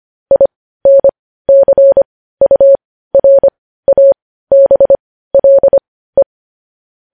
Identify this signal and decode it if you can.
Morse code, heard as a keyed tone, about 25 wpm, spaced as for 15 wpm, 550 Hz. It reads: INCURABLE
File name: podcast / News headlines-1x-25x15